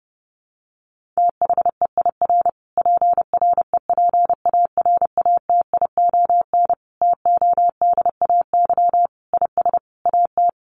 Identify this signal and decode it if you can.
Morse code, heard as a keyed tone, about 30 wpm, 700 Hz. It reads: THEIR PREPARATION TODAY IS AT
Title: Sentences from Top 1400 Words 30wpm